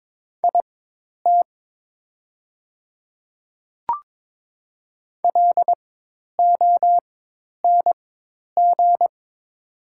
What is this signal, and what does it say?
Morse code, heard as a keyed tone, about 22 wpm, spaced as for 10 wpm, 700 Hz. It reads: IT LONG
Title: 4x Character Spacing - Top 100 Words - 22wpm